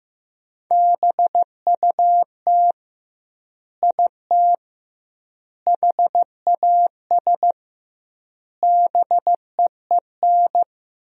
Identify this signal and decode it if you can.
Morse code, heard as a keyed tone, about 15 wpm, 700 Hz. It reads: BUT IT HAS BEEN
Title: 2x Word-Spacing - Sentences from Top 100 Words - 15wpm